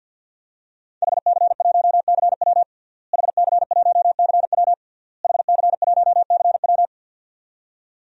Morse code, heard as a keyed tone, about 50 wpm, 700 Hz. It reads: HC1CW HC1CW HC1CW
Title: DXCC-100 Call Signs - 5 Characters - Repeated 3x - 50wpm